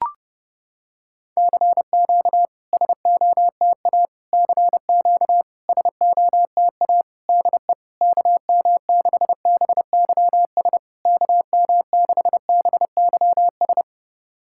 Morse code, heard as a keyed tone, about 30 wpm, 690 Hz. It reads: CQ SOTA CQ SOTA DE KM6BYS KM6BYS
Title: SOTA QSO Practice 30wpm